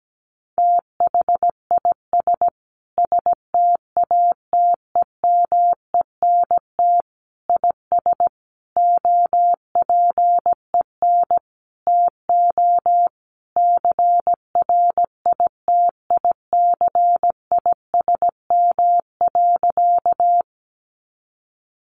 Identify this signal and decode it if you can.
Morse code, heard as a keyed tone, about 17 wpm, 700 Hz. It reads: THIS STATEMENT IS OPEN TO CRITICISM.
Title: Sentences from Top 1300 Words 17wpm